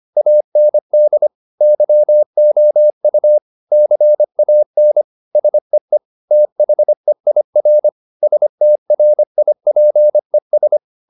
Morse code, heard as a keyed tone, about 25 wpm, 600 Hz. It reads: AND YOU CAN SEE THEIR STRIPES